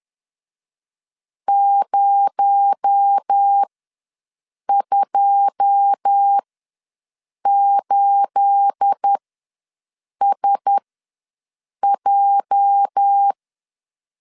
Morse code, heard as a keyed tone, about 10 wpm, spaced as for 6 wpm, 800 Hz. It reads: 028SJ
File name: LEZIONI / 9 28 ws 10wpm 4124